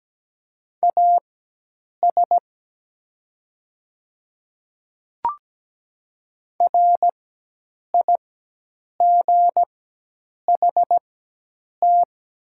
Morse code, heard as a keyed tone, about 17 wpm, spaced as for 8 wpm, 700 Hz. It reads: AS RIGHT
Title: 4x Character Spacing - Top 200 Words - 17wpm